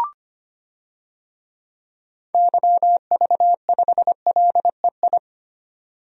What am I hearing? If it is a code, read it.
Morse code, heard as a keyed tone, about 25 wpm, 700 Hz. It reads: YV5LEI
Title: DXCC-100 Call Signs - 6 Characters - 25wpm